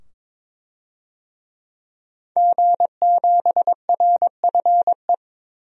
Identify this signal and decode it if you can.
Morse code, heard as a keyed tone, about 22 wpm, 700 Hz. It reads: G7RFE
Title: DXCC-100 Call Signs - 5 Characters - 22wpm